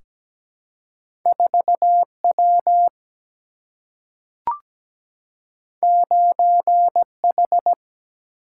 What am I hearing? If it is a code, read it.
Morse code, heard as a keyed tone, about 17 wpm, 700 Hz. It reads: 4 W 9 H
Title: Single-Letter Single-Number 17wpm